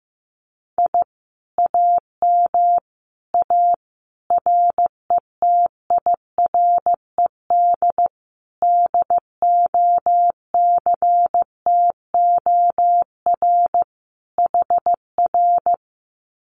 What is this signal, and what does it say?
Morse code, heard as a keyed tone, about 15 wpm, 700 Hz. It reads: I AM A RETIRED DOCTOR HR
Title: Common QSO Phrases Encore 15wpm